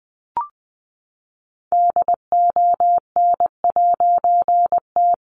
Morse code, heard as a keyed tone, about 20 wpm, 700 Hz. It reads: DON'T
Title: A Sound Of Thunder 20wpm